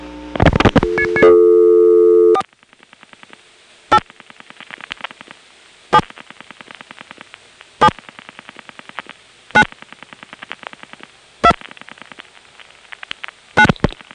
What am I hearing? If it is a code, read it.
DTMF, heard as telephone touch tones, tones of about 60 ms. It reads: 4 8 7 7 9 2 9
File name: phonetrips / SmyrnaGA5HQ